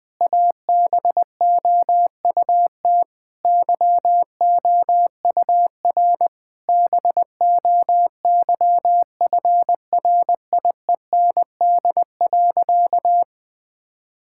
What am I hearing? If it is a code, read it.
Morse code, heard as a keyed tone, about 20 wpm, 700 Hz. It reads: ABOUT YOUR BOYFRIEND.
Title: Sentences from Top 1000 Words - Review Part 5 - 20wpm